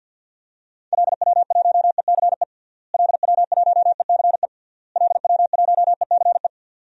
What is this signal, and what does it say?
Morse code, heard as a keyed tone, about 50 wpm, 700 Hz. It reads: LW1ECE LW1ECE LW1ECE
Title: DXCC-100 Call Signs - 6 Characters - Repeated 3x - 50wpm